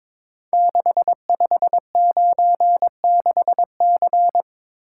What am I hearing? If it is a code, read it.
Morse code, heard as a keyed tone, about 22 wpm, 700 Hz. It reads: 6596C
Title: Code Groups - 5 Letter:Numbers - 22wpm